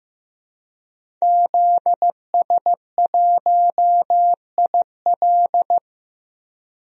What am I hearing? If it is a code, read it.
Morse code, heard as a keyed tone, about 15 wpm, 700 Hz. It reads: ZS1IL
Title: DXCC-100 Call Signs - 5 Characters - 15wpm